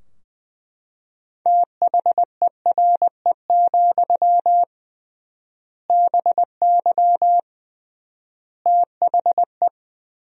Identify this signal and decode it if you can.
Morse code, heard as a keyed tone, about 21 wpm, 700 Hz. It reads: THERE, BY THE
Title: 3x Word-Spacing - Sentences from Top 500 Words - 20wpm